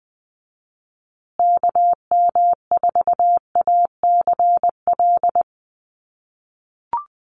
Morse code, heard as a keyed tone, about 20 wpm, 700 Hz. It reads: KM4ACL
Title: USA Call Signs Repeated 3x 20wpm